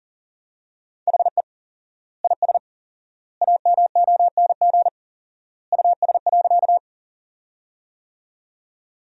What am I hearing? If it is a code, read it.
Morse code, heard as a keyed tone, about 37 wpm, 700 Hz. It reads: HE IS AMONG US.
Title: 4x Word-Spacing - Sentences from Top 500 Words - 40wpm